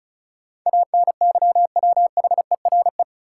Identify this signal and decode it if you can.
Morse code, heard as a keyed tone, about 36 wpm, 700 Hz. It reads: ANYWHERE
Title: Sentences from Top 1400 Words - New Word Drill - 35wpm